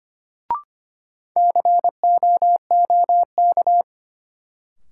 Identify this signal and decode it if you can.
Morse code, heard as a keyed tone, about 25 wpm, 700 Hz. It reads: COOK
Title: Sentences from Top 700 Words - New Word Drill - 25wpm